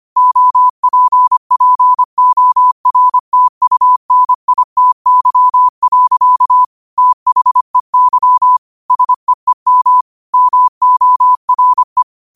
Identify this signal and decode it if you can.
Morse code, heard as a keyed tone, about 25 wpm, 1000 Hz. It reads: OPPORTUNITY. THEY SEEM MORE
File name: qotdfiles / 25wpm